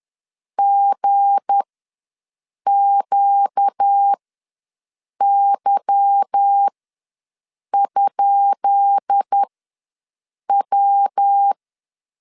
Morse code, heard as a keyed tone, about 10 wpm, spaced as for 6 wpm, 800 Hz. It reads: GQY?W
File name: LEZIONI / 10 SIGNS / 10B INT ws 10wpm 4134